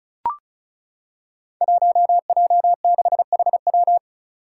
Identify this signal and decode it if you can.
Morse code, heard as a keyed tone, about 36 wpm, 700 Hz. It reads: 1J6HW
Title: Five Challenging Letter-Number Combinations 35wpm